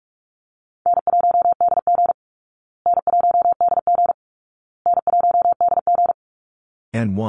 Morse code, heard as a keyed tone, about 44 wpm, 700 Hz. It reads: N1DG N1DG N1DG
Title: Call Signs Repeated 2x - 45wpm